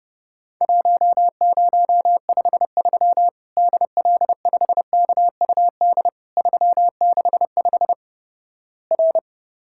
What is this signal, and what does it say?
Morse code, heard as a keyed tone, about 30 wpm, 700 Hz. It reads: 1053 DL5KUD 365 R
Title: Worked All Europe DX Contest - QTC Exchange - 30wpm